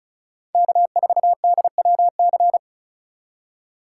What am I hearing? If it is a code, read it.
Morse code, heard as a keyed tone, about 36 wpm, 700 Hz. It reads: K4DWC
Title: USA Call Signs - 4x Character Spacing - 35wpm